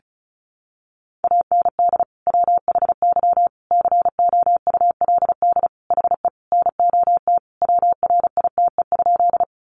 Morse code, heard as a keyed tone, about 34 wpm, 700 Hz. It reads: AND WHY COULD HE NOT WRITE?
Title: Sentences from Top 600 Words 35wpm